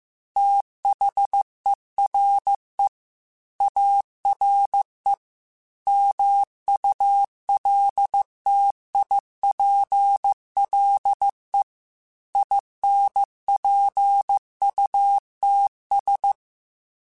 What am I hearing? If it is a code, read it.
Morse code, heard as a keyed tone, about 15 wpm, 800 Hz. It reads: THERE ARE MULTIPLE INPUTS
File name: Technical Subjects / Technical Subjects-mp3s / AND Gate @13